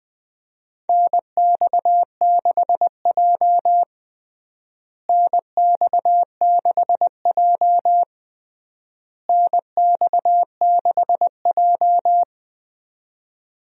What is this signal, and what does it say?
Morse code, heard as a keyed tone, about 21 wpm, 700 Hz. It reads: NX6J NX6J NX6J
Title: USA Call Signs - 4 Characters - Repeated 3x - 20wpm